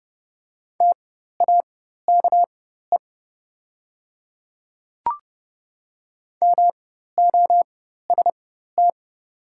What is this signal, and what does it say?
Morse code, heard as a keyed tone, about 30 wpm, spaced as for 14 wpm, 700 Hz. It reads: TAKE MOST
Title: 4x Character Spacing - Top 200 Words - 30wpm